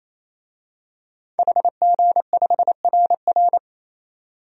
Morse code, heard as a keyed tone, about 28 wpm, 700 Hz. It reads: HG5RR
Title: DXCC-100 Call Signs - 5 Characters - 28wpm